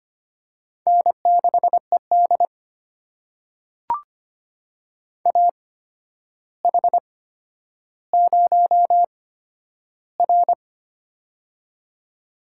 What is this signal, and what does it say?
Morse code, heard as a keyed tone, about 25 wpm, spaced as for 11 wpm, 700 Hz. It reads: N6ED AH0R
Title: USA Call Signs - 4 Characters - 8x Character Spacing - 25wpm